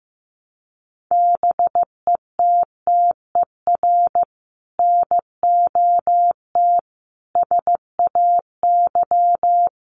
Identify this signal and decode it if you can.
Morse code, heard as a keyed tone, about 14 wpm, 700 Hz. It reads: BETTER NOT SAY
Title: Sentences from Top 1000 Words 15wpm